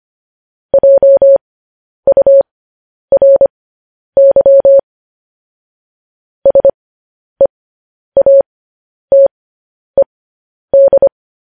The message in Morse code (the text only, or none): JURY SEATED